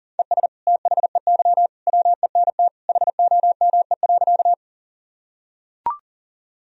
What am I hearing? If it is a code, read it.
Morse code, heard as a keyed tone, about 40 wpm, 700 Hz. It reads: ES THEY WENT HOME.
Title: Sentences from Top 200 Words and Abbreviations 40wpm